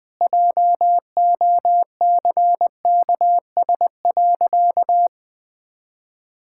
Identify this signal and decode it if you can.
Morse code, heard as a keyed tone, about 21 wpm, 700 Hz. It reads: JOCKS.
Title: Pangram Challenge v2 - 20wpm